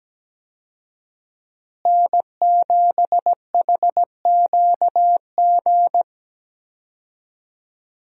Morse code, heard as a keyed tone, about 17 wpm, 700 Hz. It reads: N7HQG
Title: USA Call Signs - 5 Characters - Repeated 3x - 17wpm